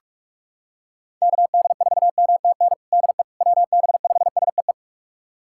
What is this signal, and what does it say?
Morse code, heard as a keyed tone, about 41 wpm, 700 Hz. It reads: KD4MTN DE WB5SEE
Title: USA Callsign DE Callsign - 6 Characters - 45wpm